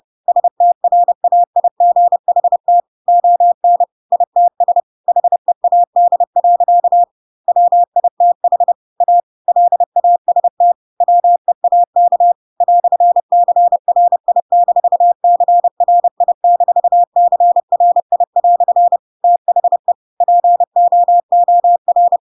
Morse code, heard as a keyed tone, about 30 wpm, 700 Hz. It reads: STRAIGHT ON ITS HEAD. WITH A LAST WEAK "CRI-CRI-CRI" THE POOR
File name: Pinocchio / Pinocchio ch4 30 30 ik1hge